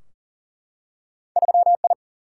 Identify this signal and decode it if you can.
Morse code, heard as a keyed tone, about 40 wpm, 700 Hz. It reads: 3 I
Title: Single-Letter Single-Number 40wpm